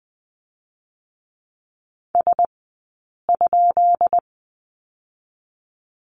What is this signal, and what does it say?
Morse code, heard as a keyed tone, about 20 wpm, 700 Hz. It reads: S ?
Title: 144 Lesson 24d - Letter y - Two Characters 20wpm